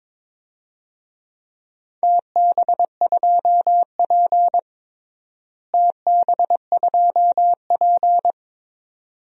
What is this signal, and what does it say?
Morse code, heard as a keyed tone, about 22 wpm, 700 Hz. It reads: TB2P TB2P
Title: DXCC-100 Call Signs - 4 Characters - Repeated 2x - 22wpm